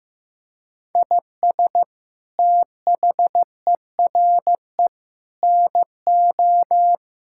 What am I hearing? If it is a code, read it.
Morse code, heard as a keyed tone, about 15 wpm, 700 Hz. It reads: IS THERE NO
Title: Sentences from Top 1000 Words - Review Part 5 - 15wpm